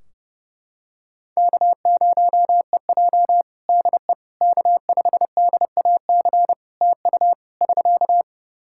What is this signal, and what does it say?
Morse code, heard as a keyed tone, about 30 wpm, 700 Hz. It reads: K0EJ DE K5DAC TU <SK>
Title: Practice QSO - Exchange 4 - Ending - 30wpm